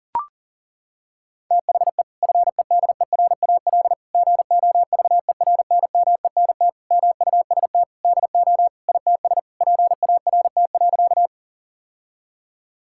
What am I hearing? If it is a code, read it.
Morse code, heard as a keyed tone, about 40 wpm, 700 Hz. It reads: THE FEDERAL GOVERNMENT MUST DO ITS PART.
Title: Sentences from Top 1000 Words - Review Part 5 - 40wpm